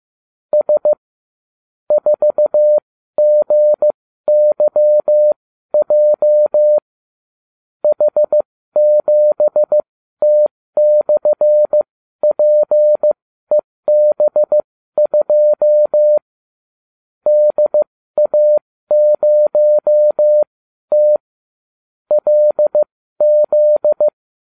Morse code, heard as a keyed tone, about 15 wpm, spaced as for 12 wpm, 600 Hz. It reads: S 4GYJ H7T/PEB2 DA0T LZ